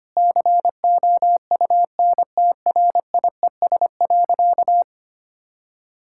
Morse code, heard as a keyed tone, about 25 wpm, 700 Hz. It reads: COUNTRIES.
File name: practice / 25wpm / Difficult Sentences with Hints - 25wpm